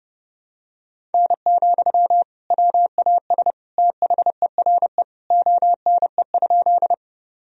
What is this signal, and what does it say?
Morse code, heard as a keyed tone, about 30 wpm, 700 Hz. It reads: N, WAS THERE ONE?
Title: Sentences from Top 100 Words and Abbreviations 30wpm